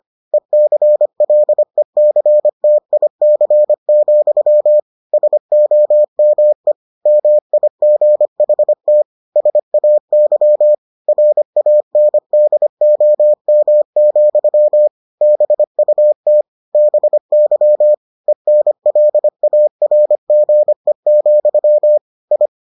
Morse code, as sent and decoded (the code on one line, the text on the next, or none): . -.-. .-.. -.-. - .. -.-. --..-- / ... --- -- . / -- .. --. .... - / ... .- -.-- / .-. .- -. -.. --- -- --..-- / -... ..- - / -... -.-- / . -. .-.. .- .-. --. --..-- / ..
ECLECTIC, SOME MIGHT SAY RANDOM, BUT BY ENLARGE, I